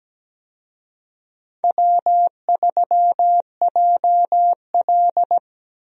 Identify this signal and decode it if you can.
Morse code, heard as a keyed tone, about 17 wpm, 700 Hz. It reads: W3JL
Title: USA Call Signs - 4 Characters - Repeated 3x - 17wpm